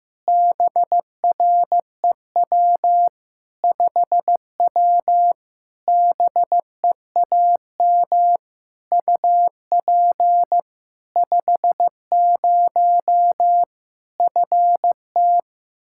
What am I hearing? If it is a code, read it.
Morse code, heard as a keyed tone, about 15 wpm, 700 Hz. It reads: BREW 5W BEAM UP 50 FT
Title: Practice QSO - Exchange 2 - Info - 15wpm